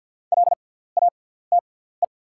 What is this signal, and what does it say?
Morse code, heard as a keyed tone, about 48 wpm, spaced as for 17 wpm, 690 Hz. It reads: LATE
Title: 6x Character Spacing - Top 300 Words - 50wpm